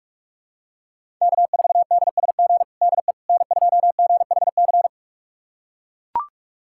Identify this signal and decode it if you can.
Morse code, heard as a keyed tone, about 44 wpm, 700 Hz. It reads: K4DSG DE N2GHC
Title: USA Callsign DE Callsign - 5 Characters - 45wpm